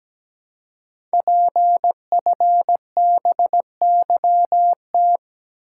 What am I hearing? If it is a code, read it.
Morse code, heard as a keyed tone, about 17 wpm, 700 Hz. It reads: PFBYT
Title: Code Groups - 5 Letters - 17wpm